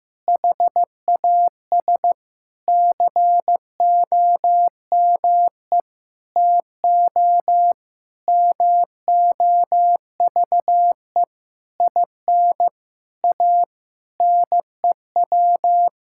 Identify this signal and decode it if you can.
Morse code, heard as a keyed tone, about 15 wpm, 700 Hz. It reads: HAS COME TO MOVE IN A NEW
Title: Sentences from Top 1000 Words - Review Part 5 - 15wpm